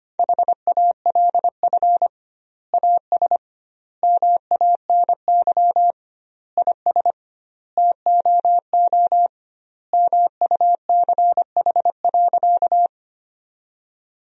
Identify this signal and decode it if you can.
Morse code, heard as a keyed tone, about 25 wpm, 700 Hz. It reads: HALF AS MANY IS TOO MUCH.
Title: 2x Word-Spacing - Sentences from Top 400 Words - 25wpm